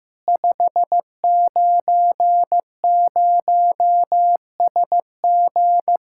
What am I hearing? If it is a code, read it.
Morse code, heard as a keyed tone, about 15 wpm, 700 Hz. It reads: 590SG
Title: Practice QSO - Exchange 2 - Info - 15wpm